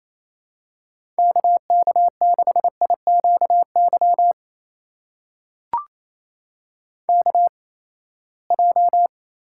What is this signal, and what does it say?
Morse code, heard as a keyed tone, about 28 wpm, spaced as for 19 wpm, 700 Hz. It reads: KK6IQY KJ
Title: USA Call Signs - 6 Characters - 8x Character Spacing - 28wpm